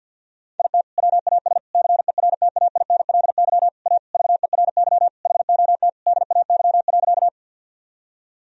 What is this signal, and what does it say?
Morse code, heard as a keyed tone, about 51 wpm, 700 Hz. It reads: IT WAS CERTAINLY A VERY HOT DAY.
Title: Sentences from Top 1000 Words - Review Part 3 - 50wpm